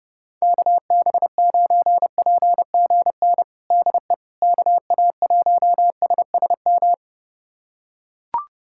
Morse code, heard as a keyed tone, about 32 wpm, 700 Hz. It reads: KB9PGN DE KA1SSM
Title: USA Callsign DE Callsign - 6 Characters - 30wpm